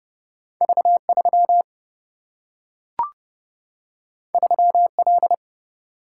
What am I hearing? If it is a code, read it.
Morse code, heard as a keyed tone, about 30 wpm, 700 Hz. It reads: V3 3L
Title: Single-Letter Single-Number 30wpm